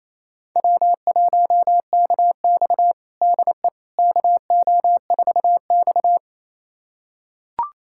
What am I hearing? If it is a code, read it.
Morse code, heard as a keyed tone, about 28 wpm, 700 Hz. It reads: W1KX DE KO4X
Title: USA Callsign DE Callsign - 4 Characters - 28wpm